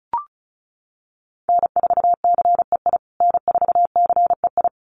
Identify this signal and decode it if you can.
Morse code, heard as a keyed tone, about 34 wpm, 700 Hz. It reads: N4CEI N4CEI
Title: USA Call Signs Repeated 2x 35wpm